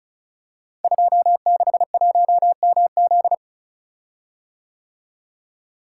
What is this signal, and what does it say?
Morse code, heard as a keyed tone, about 36 wpm, 700 Hz. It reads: 261MZ
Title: Code Groups - 5 Letter:Numbers - 35wpm